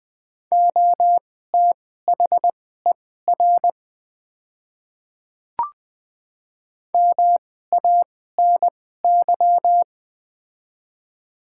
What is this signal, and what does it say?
Morse code, heard as a keyed tone, about 20 wpm, spaced as for 14 wpm, 700 Hz. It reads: OTHER MANY
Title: 2x Character Spacing - Top 200 Words - 20wpm